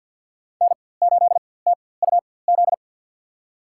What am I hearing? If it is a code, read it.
Morse code, heard as a keyed tone, about 53 wpm, spaced as for 24 wpm, 700 Hz. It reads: N8TUZ
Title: USA Call Signs - 5 Characters - 4x Character Spacing - 50wpm